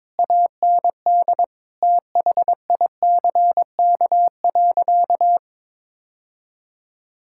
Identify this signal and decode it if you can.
Morse code, heard as a keyed tone, about 22 wpm, 700 Hz. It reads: AND THICK.
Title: Sentences from Top 1600 Words 22wpm